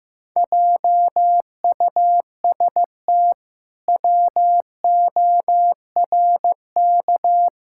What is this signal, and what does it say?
Morse code, heard as a keyed tone, about 15 wpm, 700 Hz. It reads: JUST WORK
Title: Pangram Challenge v2 - 15wpm